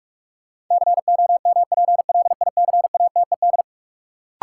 Morse code, heard as a keyed tone, about 41 wpm, 700 Hz. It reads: COMPLICATED
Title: Sentences from Top 1200 Words - New Word Drill Alt - 45wpm